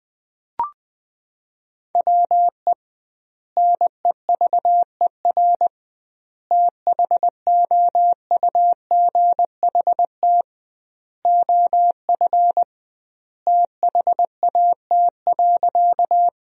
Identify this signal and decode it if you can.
Morse code, heard as a keyed tone, about 20 wpm, 700 Hz. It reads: WE NEVER THOUGHT OF THAT.
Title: 2x Word-Spacing - Sentences from Top 300 Words - 20wpm